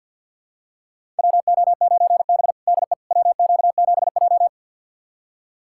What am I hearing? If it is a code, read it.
Morse code, heard as a keyed tone, about 50 wpm, 700 Hz. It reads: WO9B DE WQ7J
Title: USA Callsign DE Callsign - 4 Characters - 50wpm